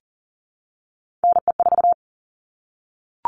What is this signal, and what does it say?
Morse code, heard as a keyed tone, about 40 wpm, 700 Hz. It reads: NE4
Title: All Cut Numbers - 3-Digit 40wpm